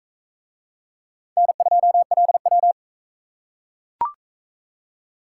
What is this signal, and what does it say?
Morse code, heard as a keyed tone, about 36 wpm, 700 Hz. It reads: N2LW